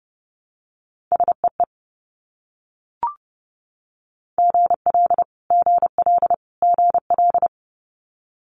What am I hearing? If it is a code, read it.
Morse code, heard as a keyed tone, about 30 wpm, 700 Hz. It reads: SEE GL GL GL
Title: Multi-letter Phonemes Repeated 3x 30wpm